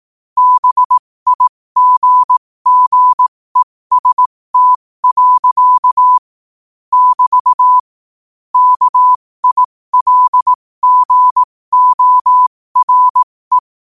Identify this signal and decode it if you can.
Morse code, heard as a keyed tone, about 18 wpm, spaced as for 15 wpm, 1000 Hz. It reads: BIGGEST. <BT> KILGORE
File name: qotdfiles / 15wpm